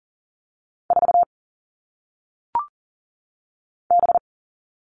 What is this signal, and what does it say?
Morse code, heard as a keyed tone, about 39 wpm, 700 Hz. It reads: 4 B
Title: All Cut Numbers - 1-Digit 40wpm